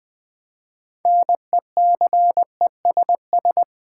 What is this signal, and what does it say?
Morse code, heard as a keyed tone, about 20 wpm, 700 Hz. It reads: NECESS
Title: Build A Word - 3 Letters At A Time - 9-Letter Words - 20wpm